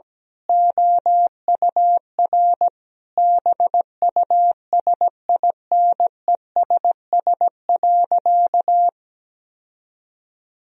Morse code, heard as a keyed tone, about 17 wpm, 700 Hz. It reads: OUR BUSINESS.
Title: Sentences from Top 1000 Words - Review Part 4 - 17wpm